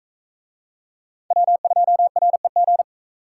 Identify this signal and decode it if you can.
Morse code, heard as a keyed tone, about 40 wpm, 700 Hz. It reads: W2REG